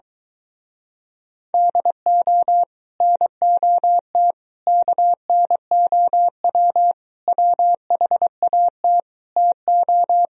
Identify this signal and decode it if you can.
Morse code, heard as a keyed tone, about 23 wpm, 700 Hz. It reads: DO NOT KNOW WHAT TO